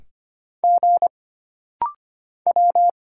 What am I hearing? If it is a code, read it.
Morse code, heard as a keyed tone, about 25 wpm, 700 Hz. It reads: G W
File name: Single Character (Letter:Number:Punctuation:Prosign) - Rapid-Fire - 25wpm